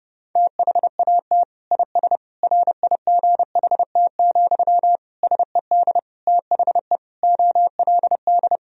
Morse code, heard as a keyed tone, about 30 wpm, 700 Hz. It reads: THAT IS RIGHT, SED THE OLD
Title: Sentences from Top 200 Words and Abbreviations 30wpm